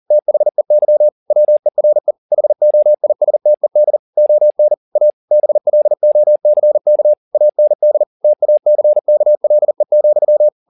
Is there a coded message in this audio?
Yes: Morse code from 0.1 to 10.5 s, heard as a keyed tone, about 40 wpm, 600 Hz. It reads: THEY WERE HOISTED ON A BLOCK AND TACKLE,